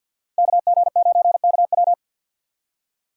Morse code, heard as a keyed tone, about 53 wpm, 700 Hz. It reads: KG9KW
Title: USA Call Signs - 6x Character Spacing - 50wpm